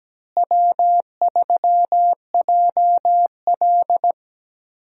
Morse code, heard as a keyed tone, about 17 wpm, 700 Hz. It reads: W3JL
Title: USA Call Signs - 4 Characters - Repeated 3x - 17wpm